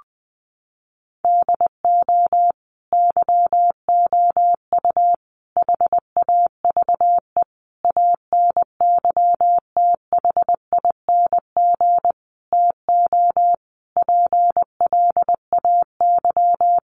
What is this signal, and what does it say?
Morse code, heard as a keyed tone, about 20 wpm, 700 Hz. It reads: DO YOU HAVE ANYTHING TO PLAY
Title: Sentences from Top 1000 Words 20wpm